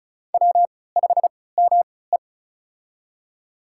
Morse code, heard as a keyed tone, about 36 wpm, spaced as for 20 wpm, 700 Hz. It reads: W5ME
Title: USA Call Signs - 4 Characters - 3x Character Spacing - 35wpm